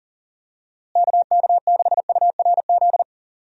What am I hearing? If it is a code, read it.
Morse code, heard as a keyed tone, about 40 wpm, 700 Hz. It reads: KK6URZ
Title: USA Call Signs - 6 Characters - Repeated 3x - 40wpm